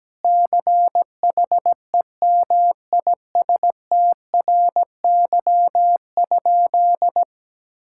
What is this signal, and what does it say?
Morse code, heard as a keyed tone, about 17 wpm, 700 Hz. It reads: CHEMISTRY?
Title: Sentences from Top 1000 Words - Review Part 6 - 17wpm